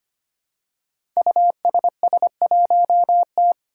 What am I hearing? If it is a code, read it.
Morse code, heard as a keyed tone, about 25 wpm, 700 Hz. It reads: USS1T
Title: Five Challenging Letter-Number Combinations 25wpm